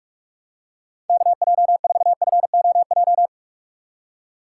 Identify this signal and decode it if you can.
Morse code, heard as a keyed tone, about 49 wpm, 700 Hz. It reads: KJ4FOJ